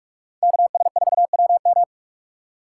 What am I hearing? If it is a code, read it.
Morse code, heard as a keyed tone, about 43 wpm, 700 Hz. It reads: KS4WM
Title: USA Call Signs - 6x Character Spacing - 45wpm